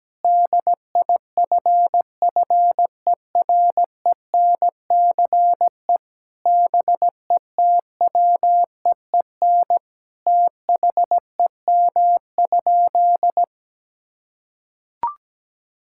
Morse code, heard as a keyed tone, about 17 wpm, 700 Hz. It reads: DIFFERENCE BETWEEN THEM?
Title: Sentences from Top 1000 Words - Review Part 5 - 17wpm